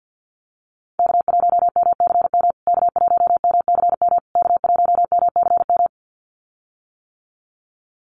Morse code, heard as a keyed tone, about 50 wpm, 700 Hz. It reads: K1MCM K1MCM K1MCM
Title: USA Call Signs Repeated 3x 50wpm